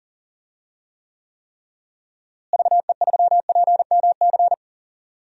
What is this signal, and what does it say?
Morse code, heard as a keyed tone, about 40 wpm, 700 Hz. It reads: VE3PMC
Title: DXCC-100 Call Signs - 6 Characters - Repeated 3x - 40wpm